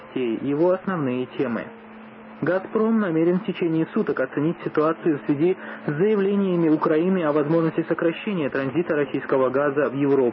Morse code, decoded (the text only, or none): none